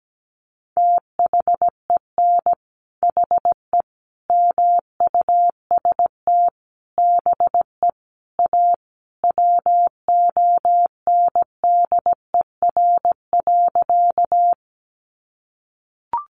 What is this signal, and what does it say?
Morse code, heard as a keyed tone, about 17 wpm, 700 Hz. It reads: THEN HE MUST BE A WONDER.
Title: Sentences from Top 500 Words 17wpm